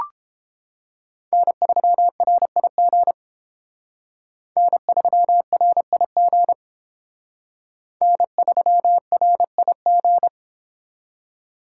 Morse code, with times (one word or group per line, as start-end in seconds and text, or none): 1.3-3.1 N3RIG
4.6-6.5 N3RIG
8.0-10.3 N3RIG